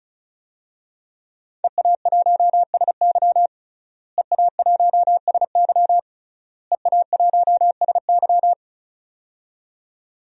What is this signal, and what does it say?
Morse code, heard as a keyed tone, about 36 wpm, 700 Hz. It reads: EA1SY EA1SY EA1SY
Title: DXCC-100 Call Signs - 5 Characters - Repeated 3x - 35wpm